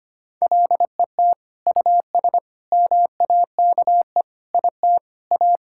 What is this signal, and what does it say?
Morse code, heard as a keyed tone, about 25 wpm, 700 Hz. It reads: LET US MAKE IT A
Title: Sentences from Top 1600 Words 25wpm